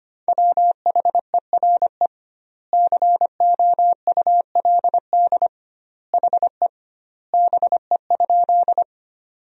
Morse code, heard as a keyed tone, about 25 wpm, 700 Hz. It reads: WHERE COULD HE BE?
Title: 2x Word-Spacing - Sentences from Top 200 Words - 25wpm